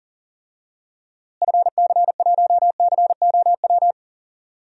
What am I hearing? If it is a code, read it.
Morse code, heard as a keyed tone, about 40 wpm, 700 Hz. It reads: FC1COW
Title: DXCC-100 Call Signs - 6 Characters - Repeated 3x - 40wpm